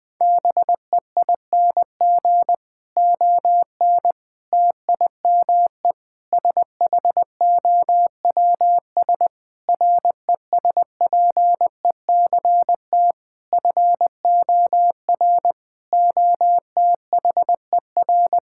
Morse code, heard as a keyed tone, about 20 wpm, 700 Hz. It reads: BEING ON TIME SHOWS RESPECT FOR OTHER
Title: Difficult Sentences with Hints - 20wpm